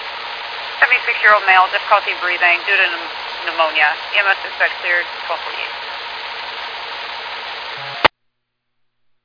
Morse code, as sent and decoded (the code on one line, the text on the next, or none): none